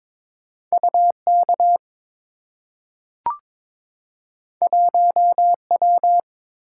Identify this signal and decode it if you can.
Morse code, heard as a keyed tone, about 23 wpm, 700 Hz. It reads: UK 1W
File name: practice / 22wpm / Two Letter-Number Combinations - 22wpm